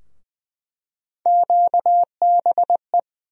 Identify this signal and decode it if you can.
Morse code, heard as a keyed tone, about 20 wpm, 700 Hz. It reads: QBE
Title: USA Call Sign Postfixes - 3 Letters 20wpm